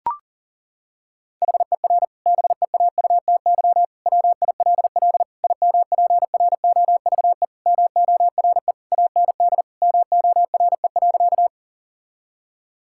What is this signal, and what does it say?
Morse code, heard as a keyed tone, about 37 wpm, 700 Hz. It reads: HER BEAUTY WILL IMPROVE MORE AND MORE.
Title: Sentences from Top 1000 Words - Review Part 5 - 40wpm